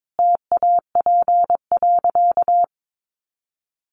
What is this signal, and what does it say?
Morse code, heard as a keyed tone, about 22 wpm, 700 Hz. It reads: TAP.
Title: Sentences from Top 1400 Words 22wpm